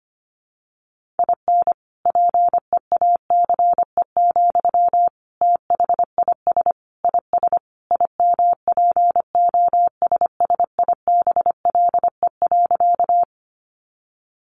Morse code, heard as a keyed tone, about 25 wpm, 700 Hz. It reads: IN PEACE, THIS IS IMPOSSIBLE.
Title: Sentences from Top 1500 Words 25wpm